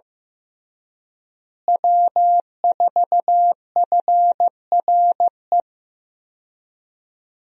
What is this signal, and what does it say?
Morse code, heard as a keyed tone, about 15 wpm, 700 Hz. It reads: W4FRE
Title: USA Call Signs - 5 Characters - Repeated 3x - 15wpm